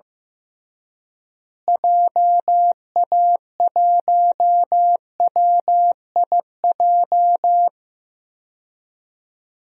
Morse code, heard as a keyed tone, about 15 wpm, 700 Hz. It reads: JA1WIJ
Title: DXCC-100 Call Signs - 6 Characters - Repeated 2x - 15wpm